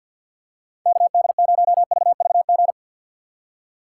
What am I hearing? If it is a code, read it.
Morse code, heard as a keyed tone, about 46 wpm, 690 Hz. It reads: KD0VVG
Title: USA Call Signs - 6 Characters - 3x Character Spacing - 50wpm